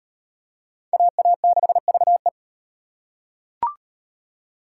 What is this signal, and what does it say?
Morse code, heard as a keyed tone, about 34 wpm, 700 Hz. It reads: AA6VE